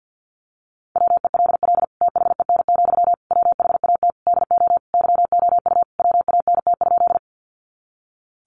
Morse code, heard as a keyed tone, about 50 wpm, 700 Hz. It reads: WELL THEN, WHAT DO YOU WANT?
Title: Sentences from Top 200 Words 50wpm